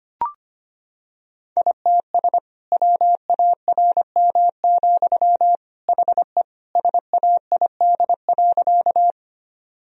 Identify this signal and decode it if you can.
Morse code, heard as a keyed tone, about 25 wpm, 700 Hz. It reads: ITS WARM, HE SAID.
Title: Sentences from Top 1000 Words - Review Part 2 - 25wpm